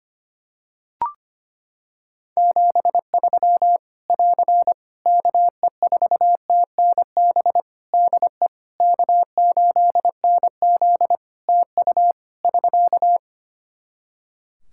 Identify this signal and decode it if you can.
Morse code, heard as a keyed tone, about 25 wpm, 700 Hz. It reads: 73 <AR> KE4TNB DE K8NZ TU <SK>
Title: Practice QSO - Exchange 4 - Ending - 25wpm